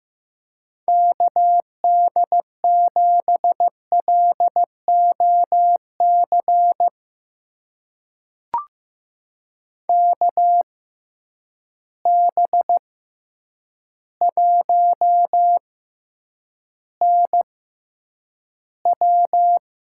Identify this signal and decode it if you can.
Morse code, heard as a keyed tone, about 15 wpm, spaced as for 8 wpm, 700 Hz. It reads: KD7LOC KB1NW
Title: USA Call Signs - 6 Characters - 6x Character Spacing - 15wpm